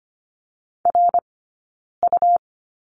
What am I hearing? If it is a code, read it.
Morse code, heard as a keyed tone, about 25 wpm, 700 Hz. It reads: R U
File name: Lesson 17d - Letter u - Two Characters 25wpm